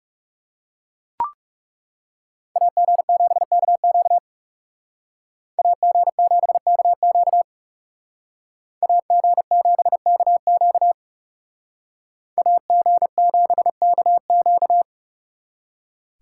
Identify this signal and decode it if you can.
Morse code, heard as a keyed tone, about 35 wpm, 700 Hz. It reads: AG7KQ AG7KQ AG7KQ AG7KQ
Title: Speed Racing - USA 2x2 Call Signs 30wpm